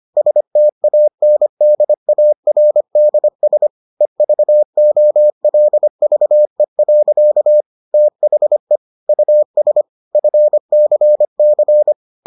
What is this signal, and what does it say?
Morse code, heard as a keyed tone, about 25 wpm, 600 Hz. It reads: STANDARDS EVOLVE. THE US FCC